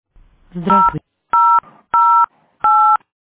DTMF, heard as telephone touch tones, tones of about 310 ms. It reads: 0008